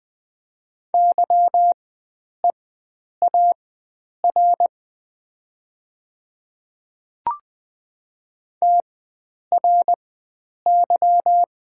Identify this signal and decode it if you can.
Morse code, heard as a keyed tone, about 20 wpm, spaced as for 9 wpm, 700 Hz. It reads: YEAR TRY
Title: 4x Character Spacing - Top 500 Words - 20wpm